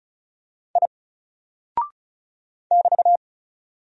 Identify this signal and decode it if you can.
Morse code, heard as a keyed tone, about 36 wpm, 700 Hz. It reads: I <BT>
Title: Single Character (Letter:Number:Punctuation:Prosign) 35wpm